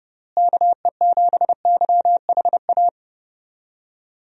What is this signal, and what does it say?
Morse code, heard as a keyed tone, about 32 wpm, 700 Hz. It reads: KE7YHA
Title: USA Call Signs - 6 Characters - Repeated 3x - 30wpm